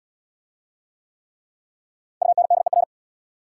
Morse code, heard as a keyed tone, about 69 wpm, 690 Hz. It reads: UNDER